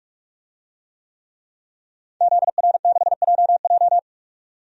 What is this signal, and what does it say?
Morse code, heard as a keyed tone, about 44 wpm, 700 Hz. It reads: ZR6JJ
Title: DXCC-100 Call Signs - 5 Characters - 45wpm